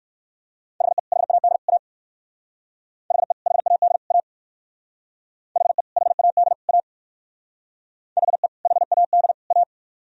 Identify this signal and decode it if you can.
Morse code, heard as a keyed tone, about 57 wpm, 700 Hz. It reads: HE HAD A HE HAD A HE HAD A HE HAD A